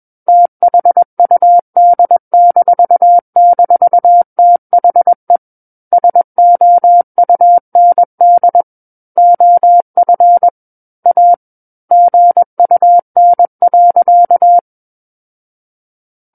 Morse code, heard as a keyed tone, about 21 wpm, 700 Hz. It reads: THUD--THE SOUND OF A GUN.
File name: Book One / 12-Chapter 12 - 21 WPM